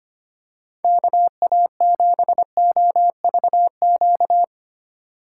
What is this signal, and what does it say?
Morse code, heard as a keyed tone, about 25 wpm, 700 Hz. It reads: KA7OVQ
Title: USA Call Signs - 6 Characters - 2x Character Spacing - 25wpm